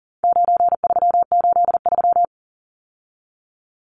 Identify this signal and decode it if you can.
Morse code, heard as a keyed tone, about 40 wpm, 700 Hz. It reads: 9383
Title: Numbers - 4-Digit 40wpm